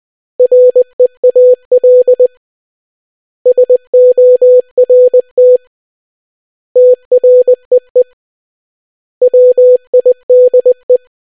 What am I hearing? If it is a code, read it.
Morse code, heard as a keyed tone, about 20 wpm, 500 Hz. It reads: REAL SORT TREE WIDE